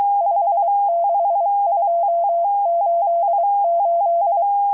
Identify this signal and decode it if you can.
Morse code, heard as a keyed tone, about 23 wpm, 680 Hz. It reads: 56288